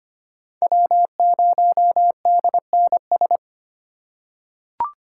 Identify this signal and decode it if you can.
Morse code, heard as a keyed tone, about 25 wpm, 700 Hz. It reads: W0DNS
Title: USA Call Signs - 5 Characters - 2x Character Spacing - 25wpm